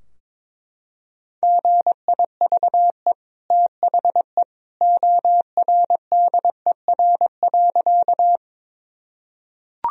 Morse code, heard as a keyed tone, about 22 wpm, 700 Hz. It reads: GIVE THE ORDER.
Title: Sentences from Top 1000 Words - Review Part 2 - 22wpm